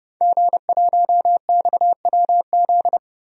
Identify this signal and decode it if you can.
Morse code, heard as a keyed tone, about 30 wpm, 700 Hz. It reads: G1XWZ
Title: DXCC-100 Call Signs - 5 Characters - Repeated 3x - 30wpm